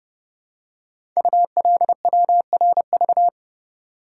Morse code, heard as a keyed tone, about 28 wpm, 700 Hz. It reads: ULWRV
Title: Code Groups - 5 Letters - 30wpm